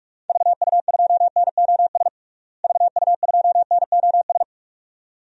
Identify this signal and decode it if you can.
Morse code, heard as a keyed tone, about 43 wpm, 700 Hz. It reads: VU2NOS VU2NOS
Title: DXCC-100 Call Signs - 6 Characters - Repeated 2x - 45wpm